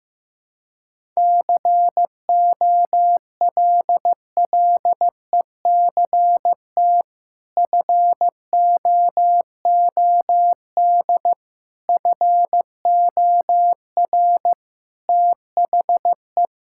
Morse code, heard as a keyed tone, about 15 wpm, 700 Hz. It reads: COLLECT FOOD FOR THE
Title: Sentences from Top 1600 Words 15wpm